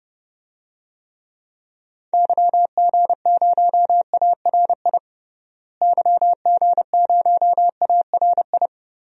Morse code, heard as a keyed tone, about 30 wpm, 700 Hz. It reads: YG0ARI YG0ARI
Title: DXCC-100 Call Signs - 6 Characters - Repeated 3x - 30wpm